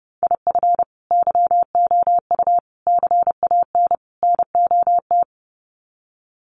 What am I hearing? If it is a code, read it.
Morse code, heard as a keyed tone, about 28 wpm, 700 Hz. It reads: IF YOU CAN NOT
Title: Sets of 4 Words 30wpm